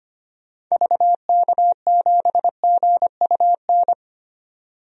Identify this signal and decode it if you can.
Morse code, heard as a keyed tone, about 25 wpm, 700 Hz. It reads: VK7GUN
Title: DXCC-100 Call Signs - 6 Characters - 25wpm